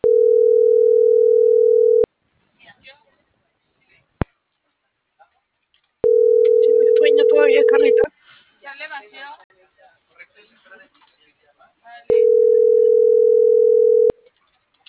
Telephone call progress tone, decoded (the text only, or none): ringback tone